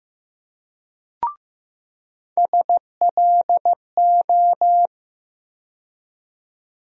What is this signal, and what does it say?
Morse code, heard as a keyed tone, about 15 wpm, 700 Hz. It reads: SLO
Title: USA Call Sign Postfixes - 3 Letters 15wpm